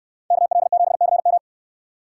Morse code, heard as a keyed tone, about 69 wpm, 700 Hz. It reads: KL7CM